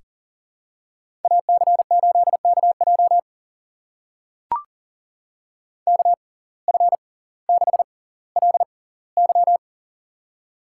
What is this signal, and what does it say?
Morse code, heard as a keyed tone, about 38 wpm, spaced as for 20 wpm, 700 Hz. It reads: AC8KJ KF6LY
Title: USA Call Signs - 5 Characters - 6x Character Spacing - 40wpm